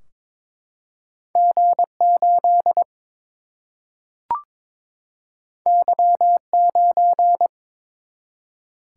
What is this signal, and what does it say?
Morse code, heard as a keyed tone, about 22 wpm, 700 Hz. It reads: G 8 Y 9
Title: Single-Letter Single-Number 22wpm